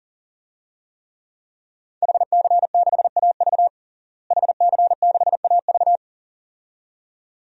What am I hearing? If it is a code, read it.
Morse code, heard as a keyed tone, about 40 wpm, 700 Hz. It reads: HC6AV HC6AV
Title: DXCC-100 Call Signs - 5 Characters - Repeated 2x - 40wpm